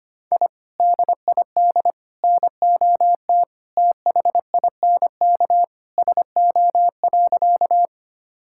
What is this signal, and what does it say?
Morse code, heard as a keyed tone, about 25 wpm, 700 Hz. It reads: I DID NOT THINK SO.
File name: practice / 25wpm / Sentences with 2-Word Phrases - 25wpm